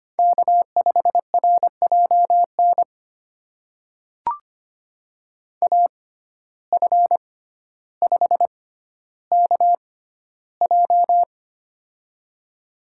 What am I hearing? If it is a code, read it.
Morse code, heard as a keyed tone, about 25 wpm, spaced as for 13 wpm, 700 Hz. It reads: K5RJN AF5KJ
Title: USA Call Signs - 5 Characters - 6x Character Spacing - 25wpm